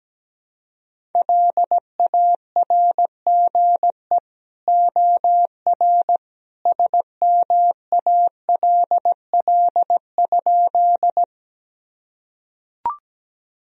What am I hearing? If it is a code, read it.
Morse code, heard as a keyed tone, about 17 wpm, 700 Hz. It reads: LARGE OR SMALL?
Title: Sentences from Top 1000 Words - Review Part 1 - 17wpm